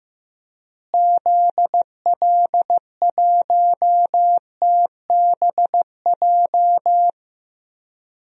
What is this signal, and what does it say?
Morse code, heard as a keyed tone, about 15 wpm, 700 Hz. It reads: ZL1TBJ
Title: DXCC-100 Call Signs - 6 Characters - Repeated 2x - 15wpm